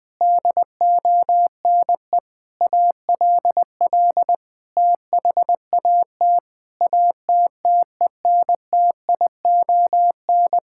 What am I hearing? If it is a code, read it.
Morse code, heard as a keyed tone, about 20 wpm, 700 Hz. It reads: DONE ALL THAT ATTENTION
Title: Sentences from Top 1000 Words - Review Part 6 - 20wpm